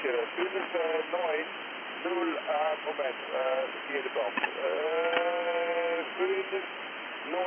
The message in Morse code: none